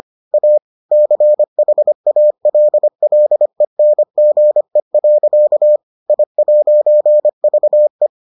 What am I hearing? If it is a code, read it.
Morse code, heard as a keyed tone, about 25 wpm, 600 Hz. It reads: A CHALLENGE. I'VE